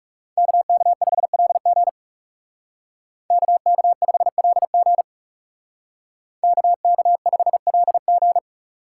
Morse code, heard as a keyed tone, about 37 wpm, 700 Hz. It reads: KK5LG KK5LG KK5LG
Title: Speed Racing - USA 2x2 Call Signs 30wpm